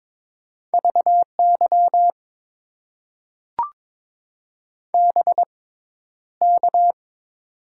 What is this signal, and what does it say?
Morse code, heard as a keyed tone, about 22 wpm, spaced as for 11 wpm, 700 Hz. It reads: VY BK
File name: Top 90 QSO Elements - 6x Character Spacing - 22wpm